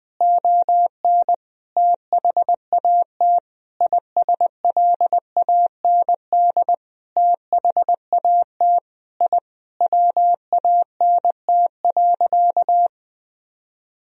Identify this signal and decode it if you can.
Morse code, heard as a keyed tone, about 20 wpm, 700 Hz. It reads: ON THAT ISLAND THAT I WANT.
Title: Sentences from Top 1000 Words - Review Part 3 - 20wpm